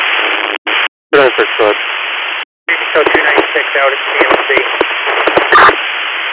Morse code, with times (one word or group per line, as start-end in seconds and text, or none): none